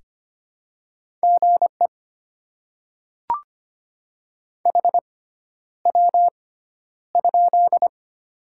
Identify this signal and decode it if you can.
Morse code, heard as a keyed tone, about 25 wpm, spaced as for 11 wpm, 700 Hz. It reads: GE HW?
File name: Top 90 QSO Elements - 6x Character Spacing - 25wpm